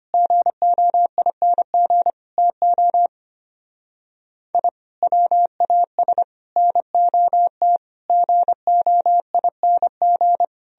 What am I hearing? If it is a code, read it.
Morse code, heard as a keyed tone, about 29 wpm, 700 Hz. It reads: GOING TO I WAS NOT GOING
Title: Speed-Racing - Sets of 5 Words 25wpm